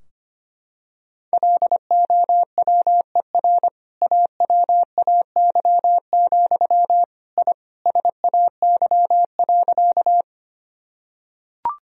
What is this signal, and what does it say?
Morse code, heard as a keyed tone, about 25 wpm, 700 Hz. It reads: LOWER AWAY, I SAY.
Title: Sentences from Top 1000 Words - Review Part 4 - 25wpm